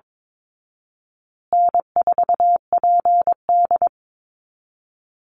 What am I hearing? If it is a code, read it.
Morse code, heard as a keyed tone, about 22 wpm, 700 Hz. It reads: N4PD
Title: Easy Call Signs Repeated 3x - 22wpm